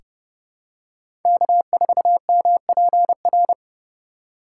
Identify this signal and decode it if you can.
Morse code, heard as a keyed tone, about 30 wpm, 700 Hz. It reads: K4MPR
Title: USA Call Signs - 5 Characters - 8x Character Spacing - 30wpm